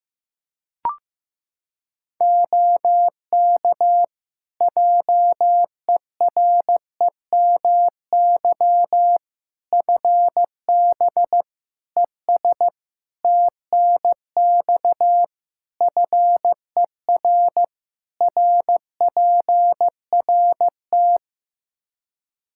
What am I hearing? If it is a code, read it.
Morse code, heard as a keyed tone, about 15 wpm, 700 Hz. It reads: OK JEREMY FB ES TNX FER RPRT
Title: Practice QSO - Exchange 2 - Info - 15wpm